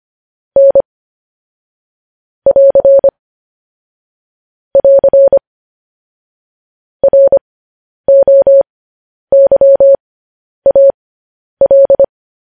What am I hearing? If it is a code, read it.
Morse code, heard as a keyed tone, about 25 wpm, spaced as for 10 wpm, 550 Hz. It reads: N <AR> <AR> ROYAL